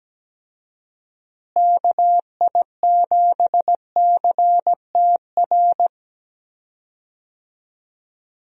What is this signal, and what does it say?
Morse code, heard as a keyed tone, about 17 wpm, 700 Hz. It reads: KI7CTR
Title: USA Call Signs - 6 Characters - Repeated 2x - 17wpm